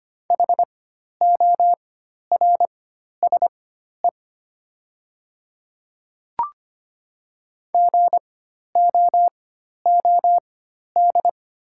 Morse code, heard as a keyed tone, about 25 wpm, spaced as for 12 wpm, 700 Hz. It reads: HORSE GOOD
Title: 4x Character Spacing - Top 300 Words - 25wpm